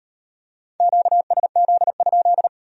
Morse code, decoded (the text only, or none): QSZ?